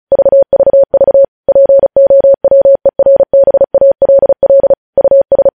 Morse code, heard as a keyed tone, about 36 wpm, 550 Hz. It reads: VVV POWERBALL US